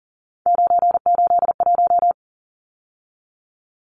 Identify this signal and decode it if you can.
Morse code, heard as a keyed tone, about 40 wpm, 700 Hz. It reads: 981
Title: Numbers - 3-Digit 40wpm